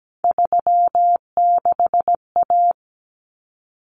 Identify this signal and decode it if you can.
Morse code, heard as a keyed tone, about 17 wpm, 700 Hz. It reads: 36A
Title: Common Cut Numbers - 3-Digit 17wpm